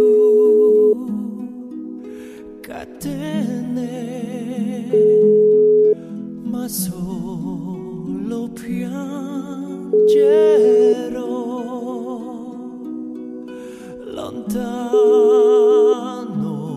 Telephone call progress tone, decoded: ringback tone

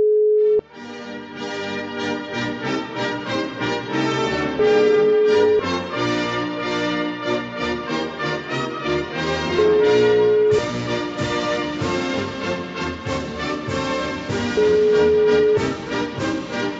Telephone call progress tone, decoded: ringback tone

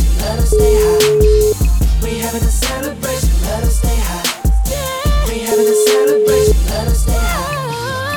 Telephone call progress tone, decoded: ringback tone

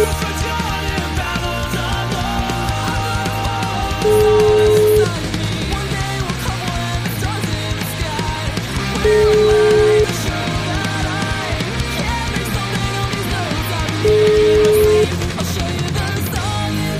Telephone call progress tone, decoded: ringback tone

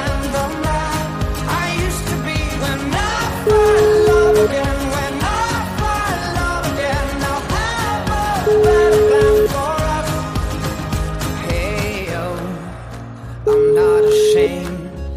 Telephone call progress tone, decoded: ringback tone

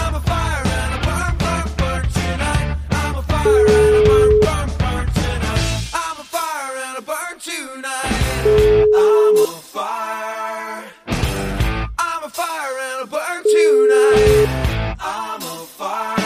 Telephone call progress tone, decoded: ringback tone

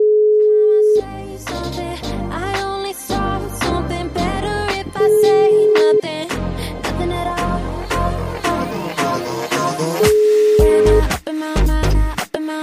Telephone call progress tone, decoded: ringback tone

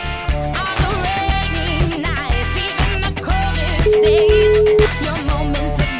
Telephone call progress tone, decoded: ringback tone

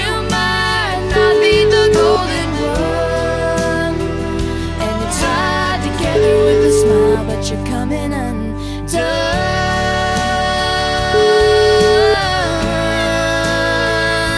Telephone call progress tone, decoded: ringback tone

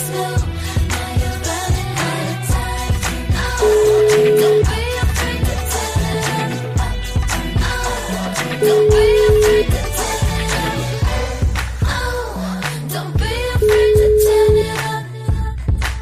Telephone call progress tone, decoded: ringback tone

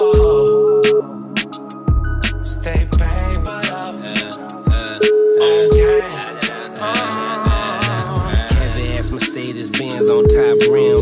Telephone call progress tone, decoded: ringback tone

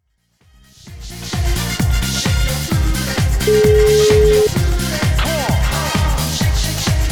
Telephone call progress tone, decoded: ringback tone

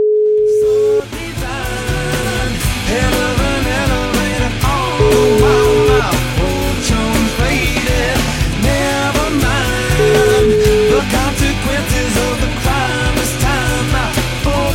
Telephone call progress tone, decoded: ringback tone

